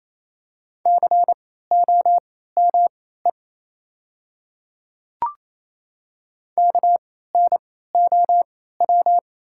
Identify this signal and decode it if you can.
Morse code, heard as a keyed tone, about 28 wpm, spaced as for 16 wpm, 700 Hz. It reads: COME KNOW
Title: 3x Character Spacing - Top 100 Words - 28wpm